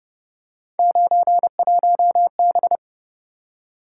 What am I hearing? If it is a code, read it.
Morse code, heard as a keyed tone, about 30 wpm, 700 Hz. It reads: 91B
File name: Speed-Racing - Three Letter-Number Combinations - 25wpm